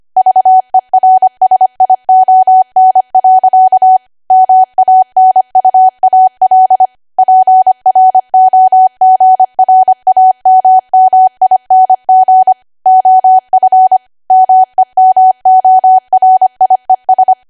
Morse code, heard as a keyed tone, about 25 wpm, 750 Hz. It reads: VERSION. MANUAL PROGRAMMING OF MEMORIES
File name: Archive / 25 WPM / 25WPM